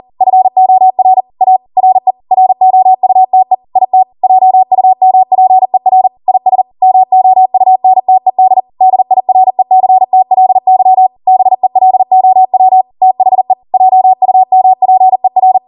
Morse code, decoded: FOR A RE ROUTE IT JUMPER IS MOUNTED DIRECTLY BELOW THE JUMPER